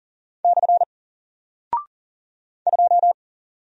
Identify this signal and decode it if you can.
Morse code, heard as a keyed tone, about 40 wpm, 700 Hz. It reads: / 2
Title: Single Character (Letter:Number:Punctuation:Prosign) 40wpm